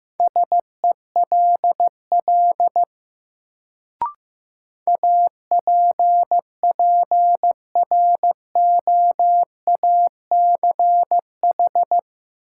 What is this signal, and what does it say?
Morse code, heard as a keyed tone, about 15 wpm, 700 Hz. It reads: SELL APPROACH
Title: Sentences from Top 1000 Words - Review Part 6 - New Word Drill - 15wpm